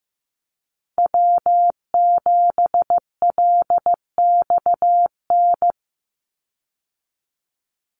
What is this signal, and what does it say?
Morse code, heard as a keyed tone, about 15 wpm, 700 Hz. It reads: W7LXN
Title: Call Signs Repeated 3x - 15wpm